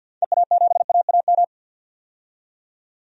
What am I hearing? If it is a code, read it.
Morse code, heard as a keyed tone, about 50 wpm, 700 Hz. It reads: EA7AAM
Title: DXCC-100 Call Signs - 6 Characters - 50wpm